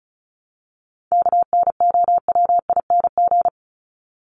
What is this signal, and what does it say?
Morse code, heard as a keyed tone, about 34 wpm, 700 Hz. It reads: KNOWING
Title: ING Suffix - Top 100 Words with Base Words 35wpm